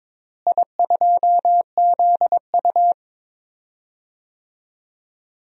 Morse code, heard as a keyed tone, about 22 wpm, 700 Hz. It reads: I2ZU
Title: DXCC-100 Call Signs - 4 Characters - Repeated 3x - 22wpm